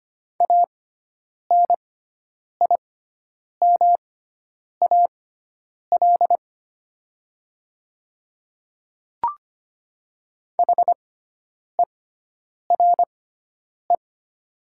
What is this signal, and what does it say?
Morse code, heard as a keyed tone, about 25 wpm, spaced as for 9 wpm, 700 Hz. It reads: ANIMAL HERE
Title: 6x Character Spacing - Top 200 Words - 25wpm